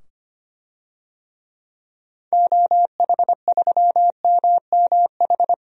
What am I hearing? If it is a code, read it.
Morse code, heard as a keyed tone, about 25 wpm, 700 Hz. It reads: OH3MMH
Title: DXCC-100 Call Signs - 6 Characters - 25wpm